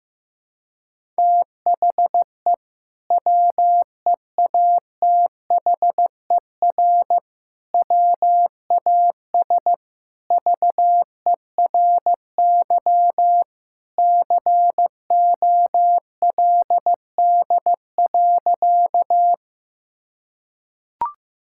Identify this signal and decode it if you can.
Morse code, heard as a keyed tone, about 15 wpm, 700 Hz. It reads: THE WEATHER WAS VERY COLD.
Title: Sentences from Top 1000 Words - Review Part 5 - 15wpm